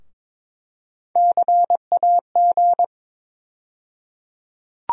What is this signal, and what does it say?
Morse code, heard as a keyed tone, about 22 wpm, 700 Hz. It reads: CAG
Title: USA Call Sign Postfixes - 3 Letters 22wpm